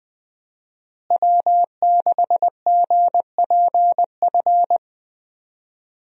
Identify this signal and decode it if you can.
Morse code, heard as a keyed tone, about 20 wpm, 700 Hz. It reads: W6GPF
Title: USA Call Signs - 5 Characters - Repeated 3x - 20wpm